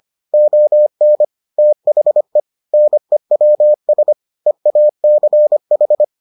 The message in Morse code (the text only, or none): ON THE NEWS EACH